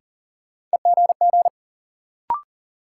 Morse code, heard as a keyed tone, about 40 wpm, 700 Hz. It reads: EGG
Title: Sentences from Top 1000 Words - Review Part 7 - New Word Drill - 40wpm